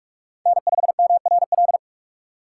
Morse code, heard as a keyed tone, about 46 wpm, 700 Hz. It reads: N5MRL
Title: USA Call Signs - 5 Characters - 4x Character Spacing - 45wpm